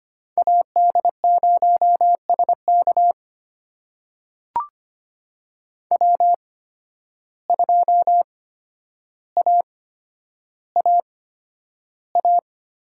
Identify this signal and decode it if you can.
Morse code, heard as a keyed tone, about 25 wpm, spaced as for 11 wpm, 700 Hz. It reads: AD0SK W2AAA
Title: USA Call Signs - 5 Characters - 8x Character Spacing - 25wpm